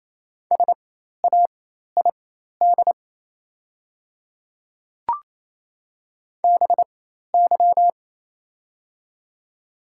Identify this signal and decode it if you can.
Morse code, heard as a keyed tone, about 28 wpm, spaced as for 13 wpm, 700 Hz. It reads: SAID BY
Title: 4x Character Spacing - Top 100 Words - 28wpm